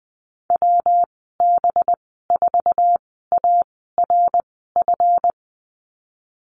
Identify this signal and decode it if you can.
Morse code, heard as a keyed tone, about 20 wpm, spaced as for 15 wpm, 700 Hz. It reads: WB4ARF
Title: 219 Lesson 35g - Number 0 - Call Signs 20wpm